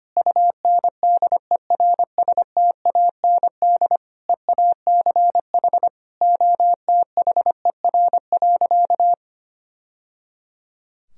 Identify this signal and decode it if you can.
Morse code, heard as a keyed tone, about 25 wpm, 700 Hz. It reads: UNDERSTAND EACH OTHER.
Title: Sentences from Top 1000 Words - Review Part 1 - 25wpm